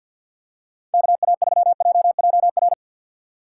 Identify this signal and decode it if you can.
Morse code, heard as a keyed tone, about 47 wpm, 700 Hz. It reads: KA3JJR
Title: USA Call Signs - 6 Characters - 3x Character Spacing - 50wpm